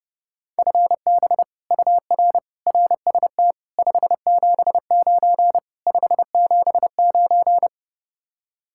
Morse code, heard as a keyed tone, about 31 wpm, 700 Hz. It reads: FB UR RST 579 579